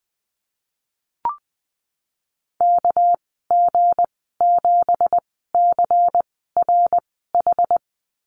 Morse code, heard as a keyed tone, about 20 wpm, spaced as for 15 wpm, 700 Hz. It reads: KG7CRH
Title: 207 Lesson 33g - Letter j - Call Signs 20wpm